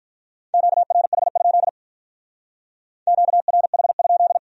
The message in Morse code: QRH? QRH?

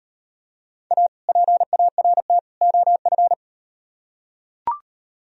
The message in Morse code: A PART OF

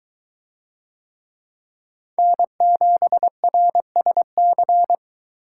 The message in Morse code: N7RSC

